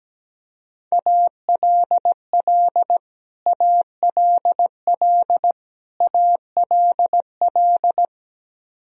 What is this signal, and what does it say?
Morse code, heard as a keyed tone, about 17 wpm, 700 Hz. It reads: ALL ALL ALL
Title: Top 100 Words Repeated 17wpm